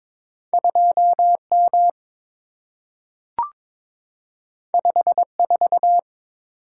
Morse code, heard as a keyed tone, about 22 wpm, 690 Hz. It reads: 2M 54
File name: practice / 22wpm / Two Letter-Number Combinations - 22wpm